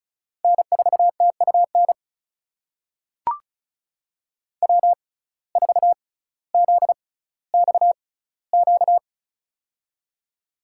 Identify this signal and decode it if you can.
Morse code, heard as a keyed tone, about 34 wpm, spaced as for 18 wpm, 700 Hz. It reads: N4TUN W4ZXQ
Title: USA Call Signs - 6x Character Spacing - 35wpm